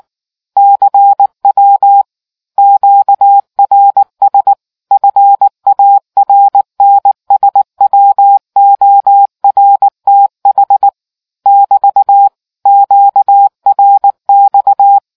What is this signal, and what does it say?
Morse code, heard as a keyed tone, about 18 wpm, 800 Hz. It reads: CW QRS FARNSWORTH <BT> QRX